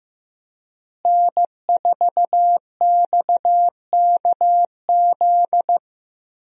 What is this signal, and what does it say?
Morse code, heard as a keyed tone, about 15 wpm, 700 Hz. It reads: N4XKZ
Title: USA Call Signs - 5 Characters - Repeated 2x - 15wpm